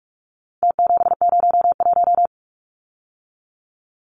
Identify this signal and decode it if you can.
Morse code, heard as a keyed tone, about 44 wpm, 700 Hz. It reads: T701
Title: Numbers - Contest Style - 4-Digit 45wpm